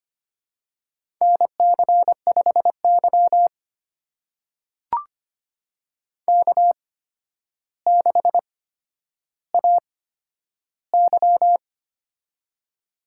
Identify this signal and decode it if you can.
Morse code, heard as a keyed tone, about 25 wpm, spaced as for 11 wpm, 700 Hz. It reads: NC5Y K6AY
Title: USA Call Signs - 4 Characters - 8x Character Spacing - 25wpm